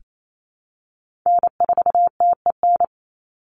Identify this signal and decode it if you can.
Morse code, heard as a keyed tone, about 28 wpm, 700 Hz. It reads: N4TEN
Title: USA Call Signs 28wpm